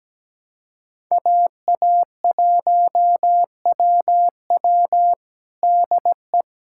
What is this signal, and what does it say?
Morse code, heard as a keyed tone, about 17 wpm, 700 Hz. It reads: AA1WW DE
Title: USA Callsign DE Callsign - 5 Characters - 17wpm